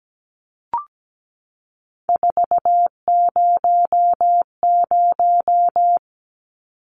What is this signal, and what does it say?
Morse code, heard as a keyed tone, about 17 wpm, 700 Hz. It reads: 400
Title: Numbers - 3-Digit 17wpm